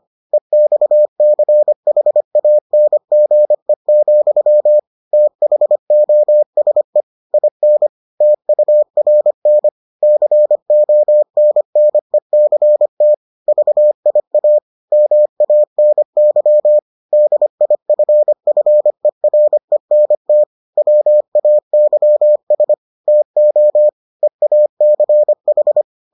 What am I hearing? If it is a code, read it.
Morse code, heard as a keyed tone, about 25 wpm, 600 Hz. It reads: EXCHANGE, THOSE IN TURN CONNECT VIA MANY DIFFERENT WAYS TO EACH